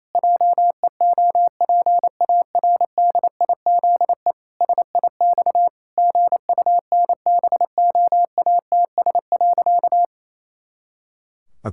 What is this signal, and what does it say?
Morse code, heard as a keyed tone, about 28 wpm, 700 Hz. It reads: JEOPARDIZE SIX GUNBOATS.
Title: Pangram Challenge v2 - 28wpm